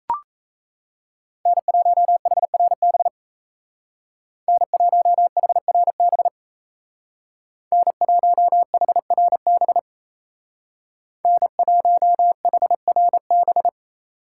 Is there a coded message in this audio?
Yes: Morse code from 1.5 to 13.7 s, heard as a keyed tone, about 37 wpm, 700 Hz. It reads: N1HRB N1HRB N1HRB N1HRB